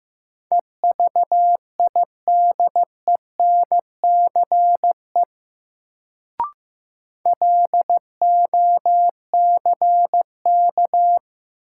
Morse code, heard as a keyed tone, about 15 wpm, 700 Hz. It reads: EVIDENCE LOCK
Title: Sentences from Top 1300 Words - New Word Drill Alt - 15wpm